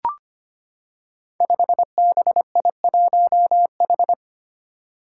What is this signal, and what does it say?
Morse code, heard as a keyed tone, about 25 wpm, 700 Hz. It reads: 5BI1H
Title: Five Challenging Letter-Number Combinations 25wpm